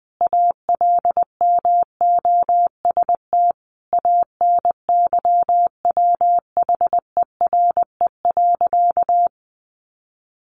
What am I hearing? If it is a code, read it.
Morse code, heard as a keyed tone, about 20 wpm, 700 Hz. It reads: ALMOST ANYWHERE.
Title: Sentences from Top 1400 Words 20wpm